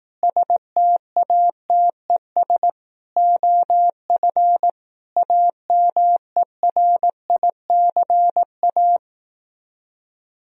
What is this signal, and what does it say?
Morse code, heard as a keyed tone, about 18 wpm, 700 Hz. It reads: STATES OF AMERICA
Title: Speed-Racing - Sets of 5 Words 15wpm